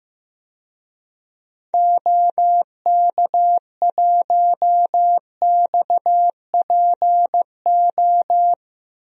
Morse code, heard as a keyed tone, about 15 wpm, 700 Hz. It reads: OK1XPO
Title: DXCC-100 Call Signs - 6 Characters - Repeated 2x - 15wpm